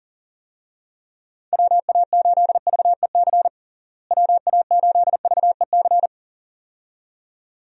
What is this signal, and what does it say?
Morse code, heard as a keyed tone, about 37 wpm, 700 Hz. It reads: WA8VEC WA8VEC
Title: USA Call Signs - 6 Characters - Repeated 2x - 40wpm